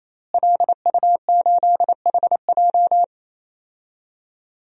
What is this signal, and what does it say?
Morse code, heard as a keyed tone, about 28 wpm, 700 Hz. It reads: LU8HJ
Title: DXCC-100 Call Signs - 5 Characters - Repeated 3x - 28wpm